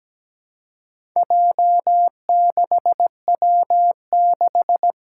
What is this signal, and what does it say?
Morse code, heard as a keyed tone, about 17 wpm, 700 Hz. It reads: J6W6
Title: Four Challenging Letter-Number Combinations 17wpm